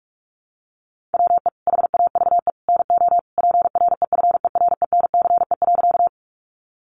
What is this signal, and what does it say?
Morse code, heard as a keyed tone, about 44 wpm, 700 Hz. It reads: WE HAVE NO PREFERENCE.
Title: Sentences from Top 1500 Words 45wpm